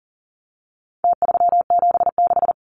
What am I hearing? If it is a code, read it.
Morse code, heard as a keyed tone, about 40 wpm, 700 Hz. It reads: T376
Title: Numbers - Contest Style - 4-Digit 40wpm